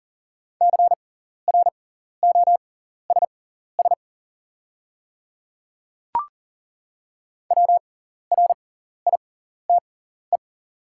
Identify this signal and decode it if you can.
Morse code, heard as a keyed tone, about 40 wpm, spaced as for 14 wpm, 700 Hz. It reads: CROSS WRITE
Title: 6x Character Spacing - Top 300 Words - 40wpm